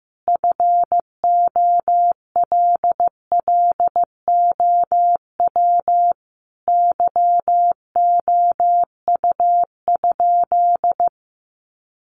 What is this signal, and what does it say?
Morse code, heard as a keyed tone, about 15 wpm, 700 Hz. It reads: FOLLOW YOU?
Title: Sentences from Top 200 Words 15wpm